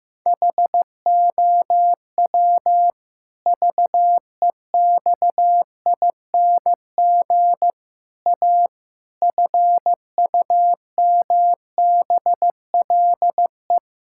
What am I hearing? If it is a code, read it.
Morse code, heard as a keyed tone, about 15 wpm, 700 Hz. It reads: HOW VEXING A FUMBLE
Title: Pangram Challenge v2 - 15wpm